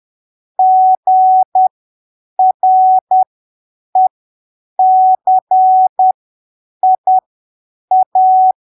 Morse code, heard as a keyed tone, about 10 wpm, spaced as for 7 wpm, 750 Hz. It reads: GRECIA
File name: arquivos / texto / texto 08 10ppm